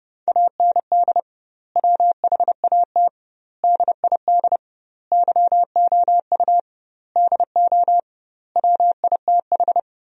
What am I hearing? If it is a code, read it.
Morse code, heard as a keyed tone, about 30 wpm, 700 Hz. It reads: AND WHAT DID YOU DO WITH
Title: 2x Word-Spacing - Sentences from Top 300 Words - 30wpm